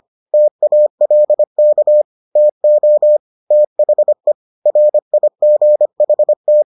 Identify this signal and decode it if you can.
Morse code, heard as a keyed tone, about 25 wpm, 600 Hz. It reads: TALK TO THE RIGHT